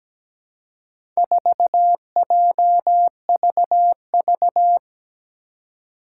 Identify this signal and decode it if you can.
Morse code, heard as a keyed tone, about 17 wpm, 700 Hz. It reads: 4JVV
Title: Four Challenging Letter-Number Combinations 17wpm